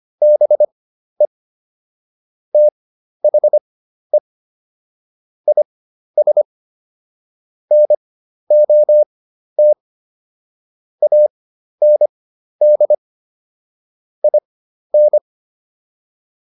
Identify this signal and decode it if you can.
Morse code, heard as a keyed tone, about 25 wpm, spaced as for 12 wpm, 600 Hz. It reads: BE THE IS NOT AND IN